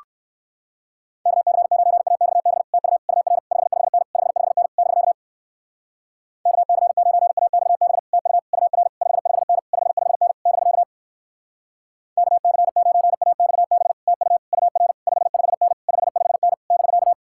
Morse code, heard as a keyed tone, about 59 wpm, 700 Hz. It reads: KC9AXB TU UR 55N 55N <BK> KC9AXB TU UR 55N 55N <BK> KC9AXB TU UR 55N 55N <BK>